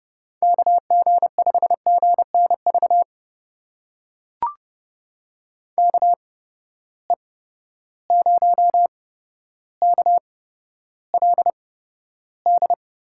Morse code, heard as a keyed tone, about 31 wpm, spaced as for 13 wpm, 700 Hz. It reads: KG5GNV KE0KLD
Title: USA Call Signs - 6 Characters - 8x Character Spacing - 30wpm